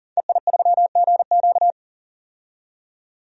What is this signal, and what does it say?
Morse code, heard as a keyed tone, about 37 wpm, 700 Hz. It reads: EI3GQ
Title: DXCC-100 Call Signs - 5 Characters - 40wpm